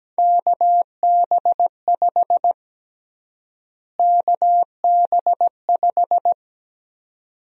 Morse code, heard as a keyed tone, about 17 wpm, 700 Hz. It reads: KB5 KB5
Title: USA Call Sign Prefixes - Repeated 2x - 17wpm